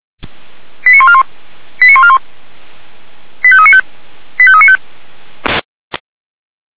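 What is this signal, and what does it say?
Five-tone selective call (ZVEI), heard as quick runs of five tones, tones of about 70 ms. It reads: 89141 89141 85395 85395
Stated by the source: ZVEI five-tone selective call received with an SDR.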